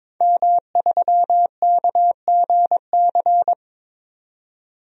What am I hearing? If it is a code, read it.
Morse code, heard as a keyed tone, about 22 wpm, 700 Hz. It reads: M3KGC
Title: DXCC-100 Call Signs - 5 Characters - 22wpm